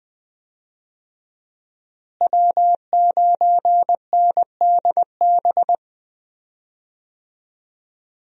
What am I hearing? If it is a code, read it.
Morse code, heard as a keyed tone, about 20 wpm, 700 Hz. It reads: W9NDB
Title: USA Call Signs - 5 Characters - Repeated 3x - 20wpm